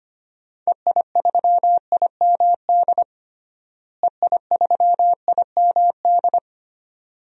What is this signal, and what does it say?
Morse code, heard as a keyed tone, about 25 wpm, 700 Hz. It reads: EI3IMD EI3IMD
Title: DXCC-100 Call Signs - 6 Characters - Repeated 2x - 25wpm